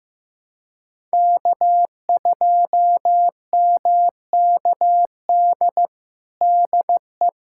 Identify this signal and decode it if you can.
Morse code, heard as a keyed tone, about 16 wpm, 700 Hz. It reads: K2MKD DE
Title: USA Callsign DE Callsign - 5 Characters - 15wpm